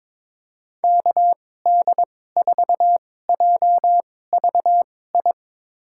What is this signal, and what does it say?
Morse code, heard as a keyed tone, about 22 wpm, spaced as for 16 wpm, 700 Hz. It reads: KD4JVI
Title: USA Call Signs - 6 Characters - 2x Character Spacing - 22wpm